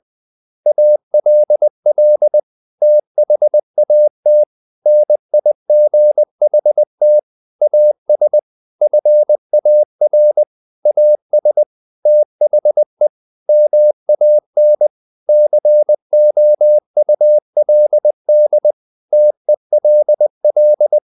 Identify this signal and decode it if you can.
Morse code, heard as a keyed tone, about 20 wpm, 600 Hz. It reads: ALL THAT NIGHT AS FAR AS THE MAN COULD TELL
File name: the old man and the sea / the old man and the sea part 1 20wpm